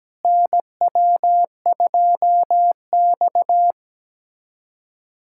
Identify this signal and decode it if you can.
Morse code, heard as a keyed tone, about 17 wpm, 700 Hz. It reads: NW2X
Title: USA Callsign DE Callsign - 4 Characters - 17wpm